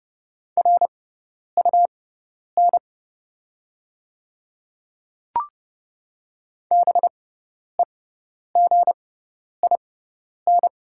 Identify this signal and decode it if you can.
Morse code, heard as a keyed tone, about 30 wpm, spaced as for 10 wpm, 700 Hz. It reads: RUN BEGIN
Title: 6x Character Spacing - Top 300 Words - 30wpm